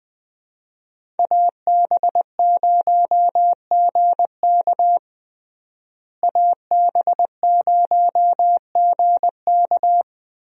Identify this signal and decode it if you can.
Morse code, heard as a keyed tone, about 20 wpm, 700 Hz. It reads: AB0GK AB0GK
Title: USA Call Signs - 5 Characters - Repeated 2x - 20wpm